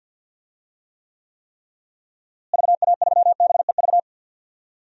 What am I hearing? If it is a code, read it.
Morse code, heard as a keyed tone, about 46 wpm, 700 Hz. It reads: VA3BEV
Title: DXCC-100 Call Signs - 6 Characters - Repeated 3x - 50wpm